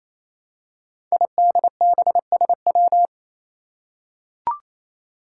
Morse code, heard as a keyed tone, about 27 wpm, 700 Hz. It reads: IDBSW
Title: Five Challenging Letter-Number Combinations 28wpm